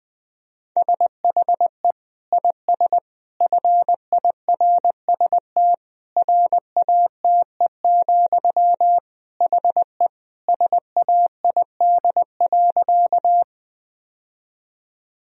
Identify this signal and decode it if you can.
Morse code, heard as a keyed tone, about 20 wpm, 700 Hz. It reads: SHE IS FIRST RATE, HE SAID.
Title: Sentences from Top 1000 Words - Review Part 4 - 20wpm